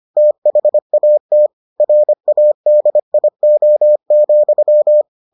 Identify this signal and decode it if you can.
Morse code, heard as a keyed tone, about 25 wpm, 600 Hz. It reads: THAT RADIO,